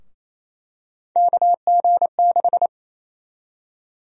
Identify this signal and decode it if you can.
Morse code, heard as a keyed tone, about 28 wpm, 700 Hz. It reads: KG6
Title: USA Call Sign Prefixes - 28wpm